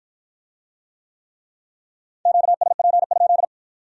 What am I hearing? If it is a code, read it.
Morse code, heard as a keyed tone, about 51 wpm, 700 Hz. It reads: QSP?